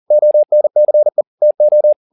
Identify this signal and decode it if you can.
Morse code, heard as a keyed tone, about 40 wpm, 600 Hz. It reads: ONCE TO